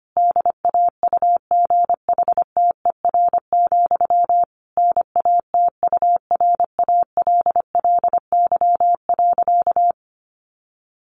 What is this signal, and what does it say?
Morse code, heard as a keyed tone, about 25 wpm, 700 Hz. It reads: DAUGHTER, NATURALLY.
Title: Sentences from Top 1500 Words 25wpm